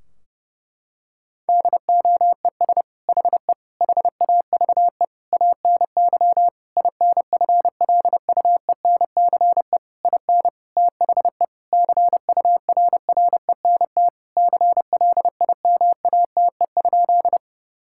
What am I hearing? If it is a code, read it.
Morse code, heard as a keyed tone, about 32 wpm, 700 Hz. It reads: DOES HE HAVE ANY INFLUENCE IN THE CURRENT CLIMATE?
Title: Sentences from Top 1600 Words 30wpm